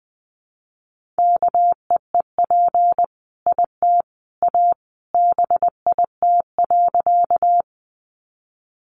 Morse code, heard as a keyed tone, about 20 wpm, 700 Hz. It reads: KEEP IT A BIT.
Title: Sentences from Top 900 Words 20wpm